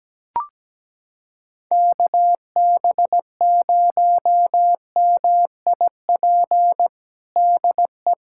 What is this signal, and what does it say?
Morse code, heard as a keyed tone, about 17 wpm, 700 Hz. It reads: KB0MIP DE
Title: USA Callsign DE Callsign 17wpm